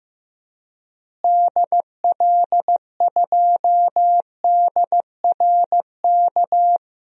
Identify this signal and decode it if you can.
Morse code, heard as a keyed tone, about 15 wpm, 700 Hz. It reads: DL2DRK
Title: DXCC-100 Call Signs - 6 Characters - Repeated 2x - 15wpm